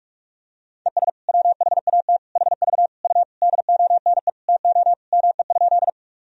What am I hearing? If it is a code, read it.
Morse code, heard as a keyed tone, about 46 wpm, 690 Hz. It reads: ES WHAT HV U DONE TO ME?
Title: Sentences from Top 500 Words and Abbreviations 45wpm